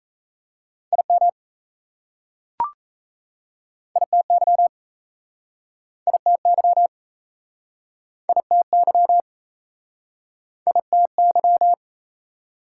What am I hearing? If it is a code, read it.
Morse code, heard as a keyed tone, about 38 wpm, 690 Hz. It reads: IM ITY ITY ITY ITY